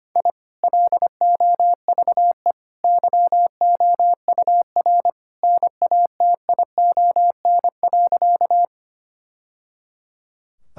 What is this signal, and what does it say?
Morse code, heard as a keyed tone, about 25 wpm, 700 Hz. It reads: I LOVE YOUR NATION.
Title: Sentences from Top 1000 Words - Review Part 6 - 25wpm